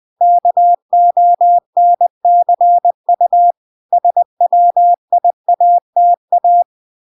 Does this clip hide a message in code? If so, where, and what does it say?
Yes: Morse code from 0.2 to 6.6 s, heard as a keyed tone, about 20 wpm, 700 Hz. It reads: KONCU SWIATA